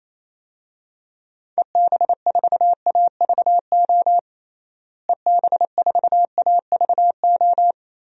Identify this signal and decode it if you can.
Morse code, heard as a keyed tone, about 28 wpm, 700 Hz. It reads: EB4AVO EB4AVO
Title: DXCC-100 Call Signs - 6 Characters - Repeated 3x - 28wpm